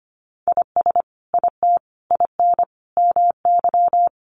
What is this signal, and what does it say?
Morse code, heard as a keyed tone, about 25 wpm, 700 Hz. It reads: IS IT IN MY
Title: Sentences from Top 1200 Words 25wpm